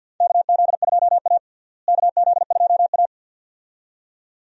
Morse code, heard as a keyed tone, about 52 wpm, 700 Hz. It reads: KZ2A KZ2A
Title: USA Call Signs - 4 Characters - Repeated 2x - 50wpm